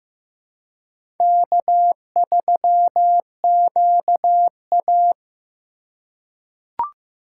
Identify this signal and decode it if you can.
Morse code, heard as a keyed tone, about 15 wpm, 700 Hz. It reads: K3QA
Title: USA Call Signs - 3x Character Spacing - 15wpm